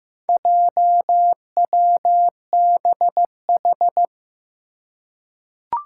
Morse code, heard as a keyed tone, about 15 wpm, 700 Hz. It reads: JWBH
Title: Four Challenging Letter-Number Combinations 15wpm